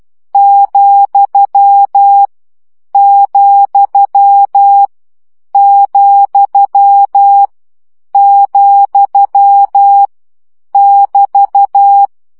Morse code, heard as a keyed tone, about 12 wpm, 800 Hz. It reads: , , , , <BT>
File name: CW gyakorlas vesszo 10 baud